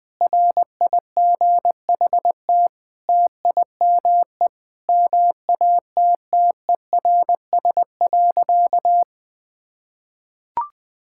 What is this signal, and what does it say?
Morse code, heard as a keyed tone, about 20 wpm, 700 Hz. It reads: RIGHT TIME MATTERS.
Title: Sentences with 9-Word Phrases - 20wpm